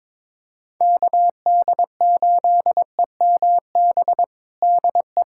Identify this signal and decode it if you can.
Morse code, heard as a keyed tone, about 22 wpm, 700 Hz. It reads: KD8EMB DE
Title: USA Callsign DE Callsign 22wpm